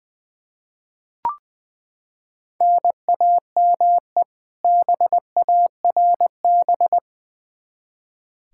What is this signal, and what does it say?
Morse code, heard as a keyed tone, about 20 wpm, 700 Hz. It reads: NAME BARB
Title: K1USN SST QSO Practice 20wpm